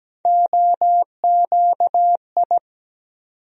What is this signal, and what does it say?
Morse code, heard as a keyed tone, about 17 wpm, 700 Hz. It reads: OQI
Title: USA Call Sign Postfixes - 3 Letters 17wpm